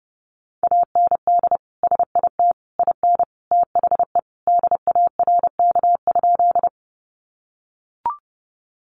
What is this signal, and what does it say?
Morse code, heard as a keyed tone, about 30 wpm, 700 Hz. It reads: AND SIT IN THE DARK?
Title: Sentences from Top 500 Words 30wpm